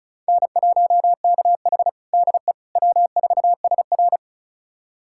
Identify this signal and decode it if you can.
Morse code, heard as a keyed tone, about 34 wpm, 700 Hz. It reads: N1KH DE W4SR
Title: USA Callsign DE Callsign - 4 Characters - 35wpm